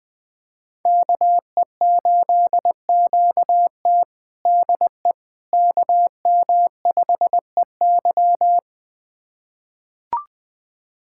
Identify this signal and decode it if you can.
Morse code, heard as a keyed tone, about 20 wpm, 700 Hz. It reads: KE8QT DE KM5EY
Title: USA Callsign DE Callsign - 5 Characters - 20wpm